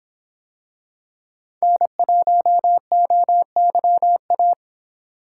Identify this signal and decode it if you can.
Morse code, heard as a keyed tone, about 26 wpm, 700 Hz. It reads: N1OYA